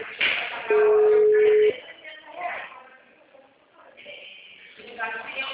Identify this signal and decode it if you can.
Ringback tone (Europe, 425 Hz), with speech.